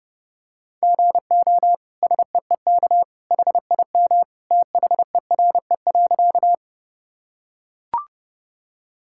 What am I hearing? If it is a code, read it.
Morse code, heard as a keyed tone, about 32 wpm, 700 Hz. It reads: GO SEEK HIM THERE.
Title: Sentences from Top 1600 Words 30wpm